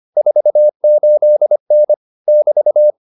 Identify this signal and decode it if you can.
Morse code, heard as a keyed tone, about 25 wpm, 600 Hz. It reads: 48N <BT>